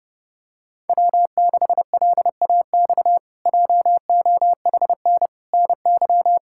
Morse code, heard as a keyed tone, about 30 wpm, 700 Hz. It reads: W6LAX JOHN NY
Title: K1USN SST QSO Practice - 30wpm